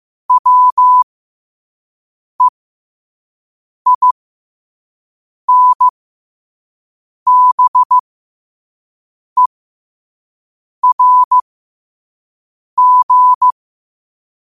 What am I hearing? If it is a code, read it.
Morse code, heard as a keyed tone, about 15 wpm, spaced as for 5 wpm, 1000 Hz. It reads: WEINBERG
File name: qotdfiles / 5wpm